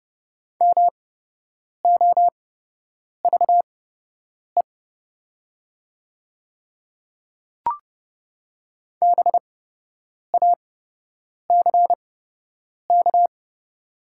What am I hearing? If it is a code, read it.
Morse code, heard as a keyed tone, about 30 wpm, spaced as for 8 wpm, 700 Hz. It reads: MOVE BACK
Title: 8x Character Spacing - Top 200 Words - 30wpm